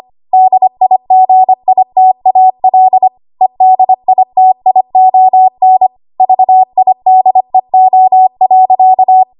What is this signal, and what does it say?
Morse code, heard as a keyed tone, about 25 wpm, 750 Hz. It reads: DIGITAL EDITION VIDEO.
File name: Archive / 25 WPM / 25WPM